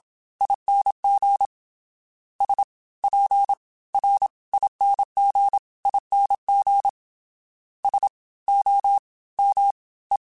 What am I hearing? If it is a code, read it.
Morse code, heard as a keyed tone, about 27 wpm, spaced as for 20 wpm, 800 Hz. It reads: ING SPRINGING SOME